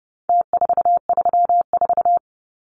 Morse code, heard as a keyed tone, about 30 wpm, 700 Hz. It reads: T434
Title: Numbers - Contest Style - 4-Digit 30wpm